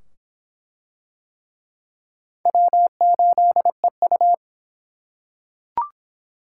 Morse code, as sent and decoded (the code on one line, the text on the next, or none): .-- ---.. ..-
W8EU